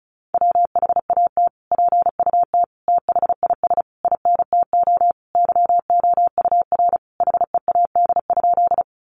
Morse code, heard as a keyed tone, about 34 wpm, 700 Hz. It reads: WHAT PUT THIS INTO YOUR HEAD?
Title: Sentences from Top 600 Words 35wpm